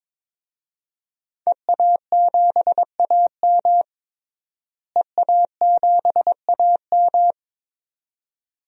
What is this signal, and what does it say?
Morse code, heard as a keyed tone, about 22 wpm, 700 Hz. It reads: EA7AM EA7AM
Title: DXCC-100 Call Signs - 5 Characters - Repeated 2x - 22wpm